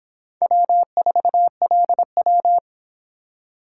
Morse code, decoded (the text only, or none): W4LW